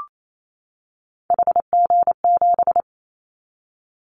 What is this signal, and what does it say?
Morse code, heard as a keyed tone, about 28 wpm, 700 Hz. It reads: HG7
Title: Callsign Prefixes for Top 50 DXCC - 28wpm